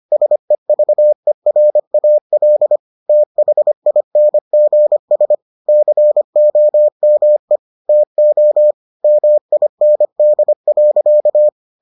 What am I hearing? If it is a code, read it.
Morse code, heard as a keyed tone, about 25 wpm, 600 Hz. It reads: SEVERAL THINGS COME TO MIND.